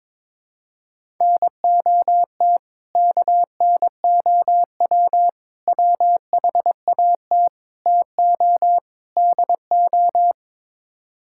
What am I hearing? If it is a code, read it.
Morse code, heard as a keyed tone, about 22 wpm, 700 Hz. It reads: NOT KNOW WHAT TO DO
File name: practice / 22wpm / Build A Phrase - 5-Word Phrases - 22wpm